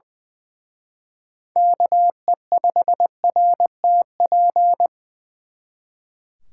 Morse code, heard as a keyed tone, about 24 wpm, 700 Hz. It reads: KE5RTP
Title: Speed Racing - USA 2x3 Call Signs 20wpm